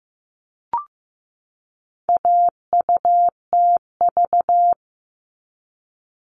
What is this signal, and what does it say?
Morse code, heard as a keyed tone, about 15 wpm, 700 Hz. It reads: AUTV
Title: All Cut Numbers - 4-Digit 15wpm